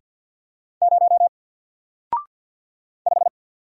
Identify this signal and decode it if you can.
Morse code, heard as a keyed tone, about 49 wpm, 700 Hz. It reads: 0 5
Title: Single Character (Letter:Number:Punctuation:Prosign) 50wpm